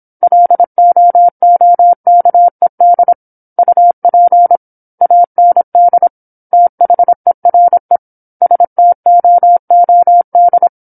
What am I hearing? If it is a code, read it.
Morse code, heard as a keyed tone, about 27 wpm, 700 Hz. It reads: LOOKED UP AND THERE STOOD
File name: 08-Chapter-08 26 WPM